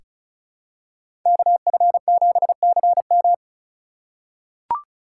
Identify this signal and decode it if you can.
Morse code, heard as a keyed tone, about 36 wpm, 700 Hz. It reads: KF7CM
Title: USA Call Signs - 5 Characters - 2x Character Spacing - 35wpm